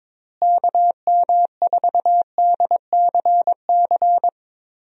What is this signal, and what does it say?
Morse code, heard as a keyed tone, about 21 wpm, 700 Hz. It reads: KM4DCC
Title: USA Call Signs - 6 Characters - Repeated 3x - 22wpm